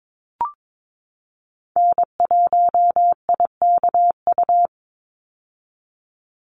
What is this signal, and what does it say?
Morse code, heard as a keyed tone, about 22 wpm, 700 Hz. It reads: N1IKU
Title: USA Call Signs 22wpm